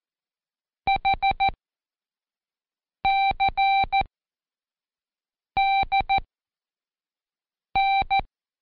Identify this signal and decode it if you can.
Morse code, heard as a keyed tone, about 14 wpm, spaced as for 5 wpm, 780 Hz. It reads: HCDN
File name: Code Quick / AudioDownld1 / Seg37-Day4-Quiz CHLN